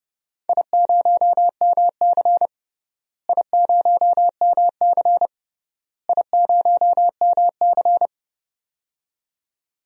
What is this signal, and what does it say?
Morse code, heard as a keyed tone, about 30 wpm, 700 Hz. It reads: I0MC I0MC I0MC
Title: DXCC-100 Call Signs - 4 Characters - Repeated 3x - 30wpm